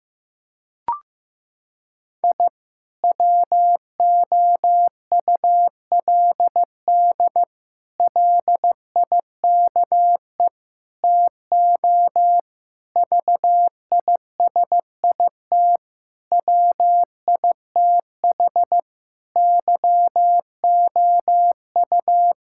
Morse code, heard as a keyed tone, about 15 wpm, 700 Hz. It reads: I WOULD LIKE TO VISIT WITH YOU
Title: Sentences from Top 1000 Words - Review Part 6 - 15wpm